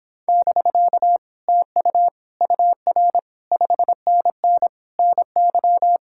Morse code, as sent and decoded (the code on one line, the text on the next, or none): -...-.- / - ..- / ..- .-. / ..... -. -. / -. -.--
<BK> TU UR 5NN NY